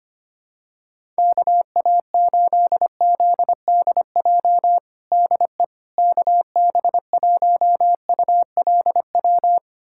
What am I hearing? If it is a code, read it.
Morse code, heard as a keyed tone, about 25 wpm, 700 Hz. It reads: KA8ZDJ DE KB1ULW
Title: USA Callsign DE Callsign 25wpm